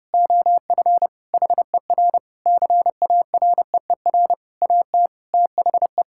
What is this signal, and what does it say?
Morse code, heard as a keyed tone, about 30 wpm, 700 Hz. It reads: OF HER CAREER AT THE
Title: Sentences from Top 1000 Words - Review Part 6 - 30wpm